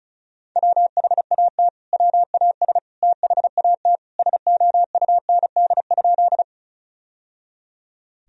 Morse code, heard as a keyed tone, about 36 wpm, 700 Hz. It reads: WHAT WAS THAT SOUND?
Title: Sentences from Top 1000 Words - Review Part 1 - 35wpm